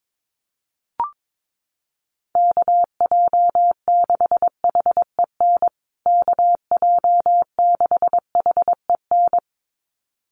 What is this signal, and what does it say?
Morse code, heard as a keyed tone, about 22 wpm, 700 Hz. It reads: KJ6HEN KJ6HEN
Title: USA Call Signs Repeated 2x 22wpm